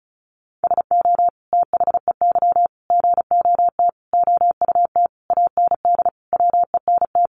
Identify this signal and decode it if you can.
Morse code, heard as a keyed tone, about 34 wpm, 700 Hz. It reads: SO THEY GOT OUT AND WENT
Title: Sentences from Top 300 Words 35wpm